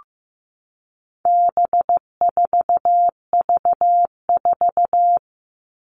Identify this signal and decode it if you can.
Morse code, heard as a keyed tone, about 15 wpm, 700 Hz. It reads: B4V4
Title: All Cut Numbers - 4-Digit 15wpm